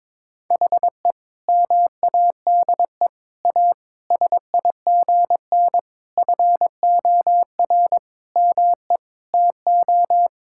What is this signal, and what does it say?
Morse code, heard as a keyed tone, about 22 wpm, 700 Hz. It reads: HE MADE A SIGN FOR ME TO
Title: Sentences from Top 1000 Words - Review Part 6 - 22wpm